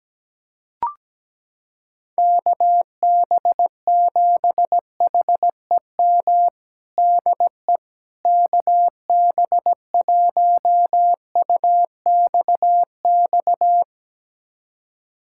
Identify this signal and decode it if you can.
Morse code, heard as a keyed tone, about 17 wpm, 700 Hz. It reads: KB7HEM DE KB1UXX
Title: USA Callsign DE Callsign - 6 Characters - 17wpm